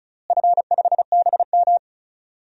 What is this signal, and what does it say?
Morse code, heard as a keyed tone, about 36 wpm, 700 Hz. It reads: F5BM
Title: DXCC-100 Call Signs - 4 Characters - Repeated 3x - 35wpm